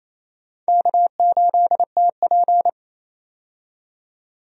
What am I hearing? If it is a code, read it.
Morse code, heard as a keyed tone, about 27 wpm, 690 Hz. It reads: K8TP
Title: USA Call Signs - 4 Characters - 28wpm